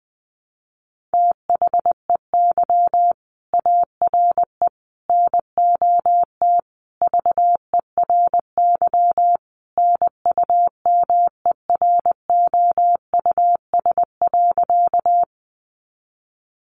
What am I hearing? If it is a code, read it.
Morse code, heard as a keyed tone, about 20 wpm, 700 Hz. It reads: THEY ARE NOT VERY NUMEROUS.
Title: Sentences from Top 1400 Words 20wpm